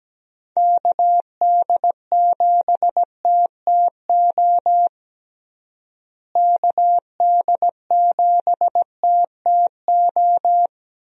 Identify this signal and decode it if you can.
Morse code, heard as a keyed tone, about 17 wpm, 700 Hz. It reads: KD7TTO KD7TTO
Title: USA Call Signs - 6 Characters - Repeated 3x - 17wpm